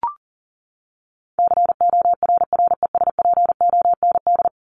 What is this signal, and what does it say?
Morse code, heard as a keyed tone, about 40 wpm, 700 Hz. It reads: CORRESPOND
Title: ING Suffix - Top 100 Words with Base Words 40wpm